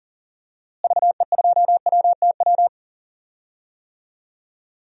Morse code, heard as a keyed tone, about 40 wpm, 700 Hz. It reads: VE2WTW
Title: DXCC-100 Call Signs - 6 Characters - Repeated 2x - 40wpm